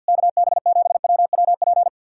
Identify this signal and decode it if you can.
Morse code, heard as a keyed tone, about 66 wpm, spaced as for 57 wpm, 700 Hz. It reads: KB7WWP
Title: Speed-Racing Common QSO Phrases - 50wpm